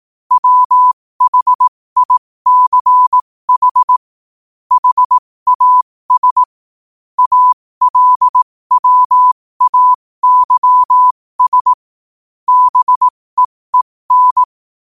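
Morse code, heard as a keyed tone, about 18 wpm, spaced as for 15 wpm, 1000 Hz. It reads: WHICH HAS ALWAYS BEEN
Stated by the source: Morse code quote of the day sent at 15 words per minute.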